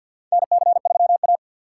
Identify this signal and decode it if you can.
Morse code, heard as a keyed tone, about 47 wpm, 700 Hz. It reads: NC3A
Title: USA Call Signs - 2x Character Spacing - 50wpm